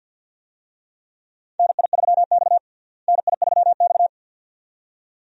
Morse code, heard as a keyed tone, about 52 wpm, 700 Hz. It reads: NI3X NI3X
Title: USA Call Signs - 4 Characters - Repeated 2x - 50wpm